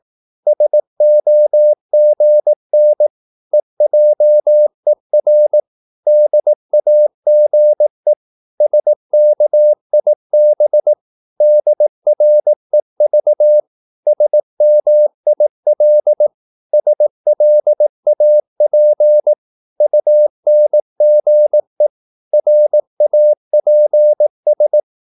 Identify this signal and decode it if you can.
Morse code, heard as a keyed tone, about 18 wpm, 600 Hz. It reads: SOGN EJER DAGE SKIB DREV SMIL SLAP UNGE RAPS
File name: Korte ord DK 18wpm